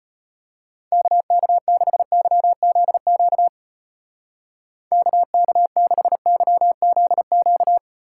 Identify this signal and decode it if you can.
Morse code, heard as a keyed tone, about 32 wpm, 700 Hz. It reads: KK6YZQ KK6YZQ